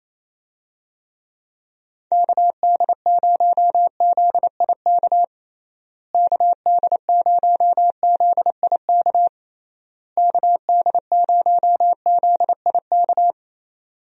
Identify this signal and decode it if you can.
Morse code, heard as a keyed tone, about 28 wpm, 700 Hz. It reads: KD0ZIK KD0ZIK KD0ZIK
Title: USA Call Signs - 6 Characters - Repeated 3x - 28wpm